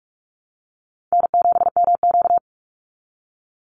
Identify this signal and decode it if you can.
Morse code, heard as a keyed tone, about 44 wpm, 700 Hz. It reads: N7MQ
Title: Call Signs Repeated 3x - 45wpm